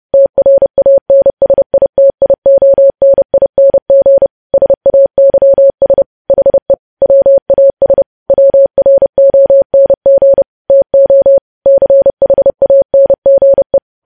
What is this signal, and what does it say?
Morse code, heard as a keyed tone, about 30 wpm, 550 Hz. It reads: TRANSITIONING SAYS HE WAS WRONG TO CHANGE
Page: news headlines in morse code at 30 wpm